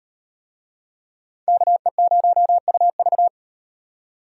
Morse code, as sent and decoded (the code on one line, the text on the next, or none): -.- . ----- ..- ...-
KE0UV